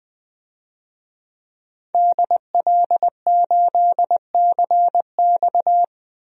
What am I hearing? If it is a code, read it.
Morse code, heard as a keyed tone, about 21 wpm, 700 Hz. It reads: DL8CX
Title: DXCC-100 Call Signs - 5 Characters - 20wpm